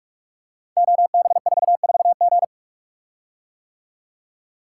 Morse code, heard as a keyed tone, about 44 wpm, 700 Hz. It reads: OB44G
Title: Code Groups - 5 Letter:Numbers - 45wpm